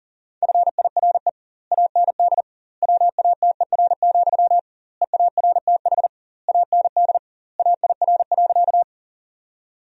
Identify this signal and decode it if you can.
Morse code, heard as a keyed tone, about 40 wpm, 700 Hz. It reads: FIRE AND WATER, EARTH AND AIR.
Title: 2x Word-Spacing - Sentences from Top 400 Words - 40wpm